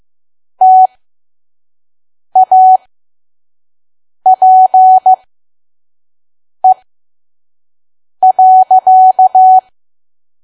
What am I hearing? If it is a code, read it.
Morse code, heard as a keyed tone, about 15 wpm, spaced as for 5 wpm, 750 Hz. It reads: TAPE.
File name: Archive / 5 WPM files / PM